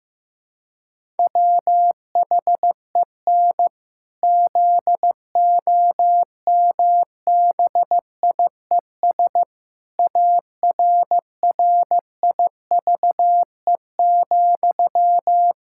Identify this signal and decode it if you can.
Morse code, heard as a keyed tone, about 15 wpm, 700 Hz. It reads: WHEN ZOMBIES ARRIVE,
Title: Pangram Challenge v2 - 15wpm